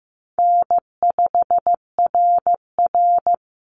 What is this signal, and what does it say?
Morse code, heard as a keyed tone, about 15 wpm, 700 Hz. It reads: N5RR
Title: Easy Call Signs - 15wpm